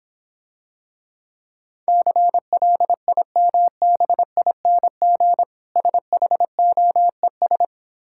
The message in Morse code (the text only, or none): CLIMBING SHOES